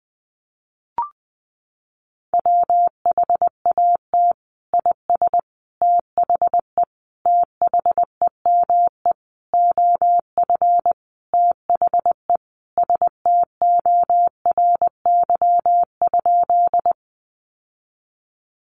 Morse code, heard as a keyed tone, about 20 wpm, 700 Hz. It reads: WHAT IS THE THEME OF THE STORY?
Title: Sentences from Top 1100 Words 20wpm